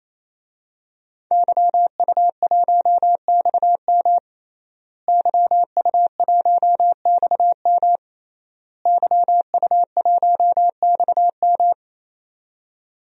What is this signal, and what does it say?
Morse code, heard as a keyed tone, about 28 wpm, 700 Hz. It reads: YU1XM YU1XM YU1XM
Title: DXCC-100 Call Signs - 5 Characters - Repeated 3x - 28wpm